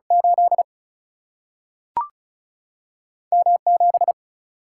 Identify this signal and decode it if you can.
Morse code, heard as a keyed tone, about 36 wpm, 700 Hz. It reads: 8 M 7
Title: Single-Letter Single-Number 35wpm